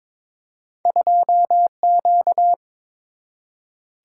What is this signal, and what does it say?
Morse code, heard as a keyed tone, about 22 wpm, 700 Hz. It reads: 2Q
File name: practice / 22wpm / Two Letter-Number Combinations - 22wpm